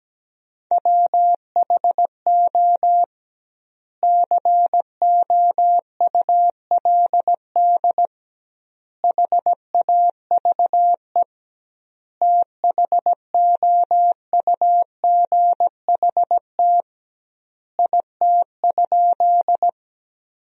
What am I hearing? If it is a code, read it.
Morse code, heard as a keyed tone, about 17 wpm, 700 Hz. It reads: WHO COULD HAVE THOUGHT IT?
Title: 2x Word-Spacing - Sentences from Top 300 Words - 17wpm